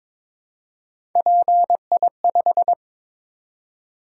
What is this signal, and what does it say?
Morse code, heard as a keyed tone, about 22 wpm, 700 Hz. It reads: PI5
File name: practice / 22wpm / Three Letter-Number Combinations - 22wpm